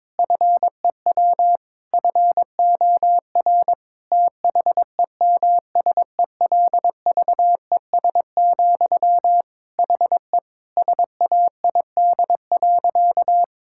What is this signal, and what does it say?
Morse code, heard as a keyed tone, about 22 wpm, 700 Hz. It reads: FEW FOR THEMSELVES, HE SAID.
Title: Sentences from Top 1000 Words - Review Part 6 - 22wpm